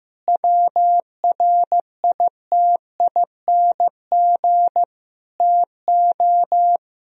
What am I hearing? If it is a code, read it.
Morse code, heard as a keyed tone, about 16 wpm, 700 Hz. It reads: WRITING TO
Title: Sentences from Top 1000 Words - Review Part 4 - 15wpm